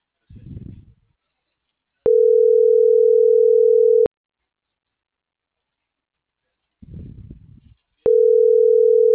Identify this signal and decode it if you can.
Ringback tone (North America), 440 + 480 Hz.